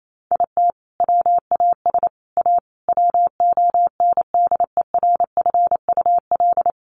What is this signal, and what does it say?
Morse code, heard as a keyed tone, about 28 wpm, 700 Hz. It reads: IT WAS A WONDERFUL
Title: Sentences from Top 1500 Words 28wpm